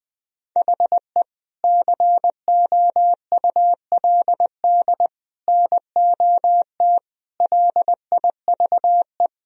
Morse code, heard as a keyed tone, about 20 wpm, 700 Hz. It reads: HE COULD NOT LIVE
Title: Sentences from Top 1000 Words - Review Part 3 - 20wpm